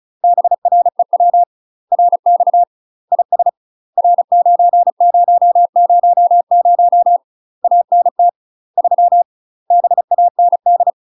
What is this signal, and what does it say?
Morse code, heard as a keyed tone, about 34 wpm, 700 Hz. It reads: BREW RX IS R9000 ANT 3 BAND
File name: UK MASTER QSOs TRAINING / 35WPM / UK MASTER QSOs 0025